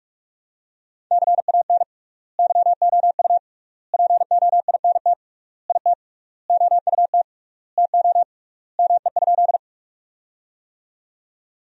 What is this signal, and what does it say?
Morse code, heard as a keyed tone, about 42 wpm, 700 Hz. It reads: CAN YOU POINT IT OUT TO ME?
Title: 3x Word-Spacing - Sentences from Top 200 Words - 45wpm